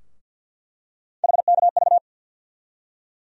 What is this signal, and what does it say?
Morse code, heard as a keyed tone, about 47 wpm, 700 Hz. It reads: HKV
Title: USA Call Sign Postfixes - 3 Letters 50wpm